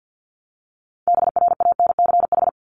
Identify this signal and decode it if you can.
Morse code, heard as a keyed tone, about 50 wpm, 700 Hz. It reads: BRANCH
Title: Top 1751-2000 Words 50wpm